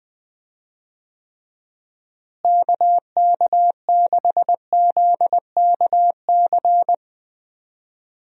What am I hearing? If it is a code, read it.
Morse code, heard as a keyed tone, about 20 wpm, 700 Hz. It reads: KK6ZKC
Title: USA Call Signs - 6 Characters - Repeated 3x - 20wpm